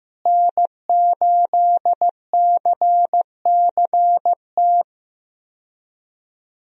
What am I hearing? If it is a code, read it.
Morse code, heard as a keyed tone, about 15 wpm, 700 Hz. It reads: N8CCT
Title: USA Call Signs - 5 Characters - Repeated 3x - 15wpm